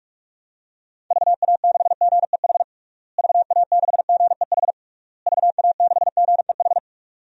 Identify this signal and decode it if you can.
Morse code, heard as a keyed tone, about 46 wpm, 700 Hz. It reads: VA6GEH VA6GEH VA6GEH
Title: DXCC-100 Call Signs - 6 Characters - Repeated 3x - 45wpm